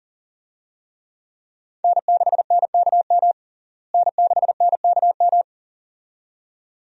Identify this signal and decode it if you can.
Morse code, heard as a keyed tone, about 40 wpm, 700 Hz. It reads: N6NKM N6NKM
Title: USA Call Signs - 5 Characters - Repeated 2x - 40wpm